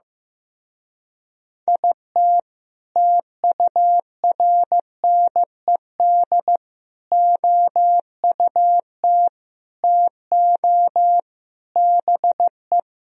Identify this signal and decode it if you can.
Morse code, heard as a keyed tone, about 15 wpm, 700 Hz. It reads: IT TURNED OUT TO BE
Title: Speed-Racing - Sets of 5 Words 15wpm